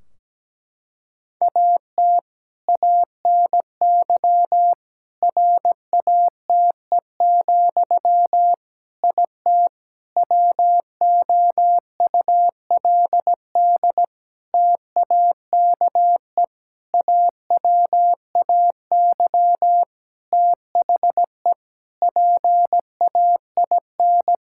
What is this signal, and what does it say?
Morse code, heard as a keyed tone, about 17 wpm, 700 Hz. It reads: AT ANY RATE, IT WOULD TAKE AWAY THE PAIN
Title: Sentences from Top 1600 Words 17wpm